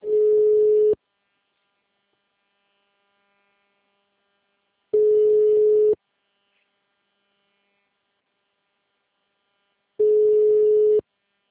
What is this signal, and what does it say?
Ringback tone (Europe, 425 Hz).
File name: out- -2022- - - .176